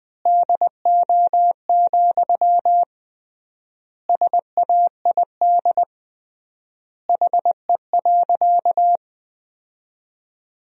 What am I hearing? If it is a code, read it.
Morse code, heard as a keyed tone, about 19 wpm, 700 Hz. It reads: DO, SAID HE.
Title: 3x Word-Spacing - Sentences from Top 100 Words - 20wpm